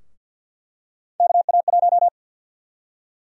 Morse code, heard as a keyed tone, about 50 wpm, 700 Hz. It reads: KA1
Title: USA Call Sign Prefixes - 50wpm